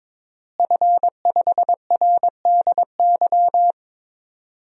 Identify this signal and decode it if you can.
Morse code, heard as a keyed tone, about 22 wpm, 700 Hz. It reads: F5RDY
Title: DXCC-100 Call Signs - 5 Characters - Repeated 3x - 22wpm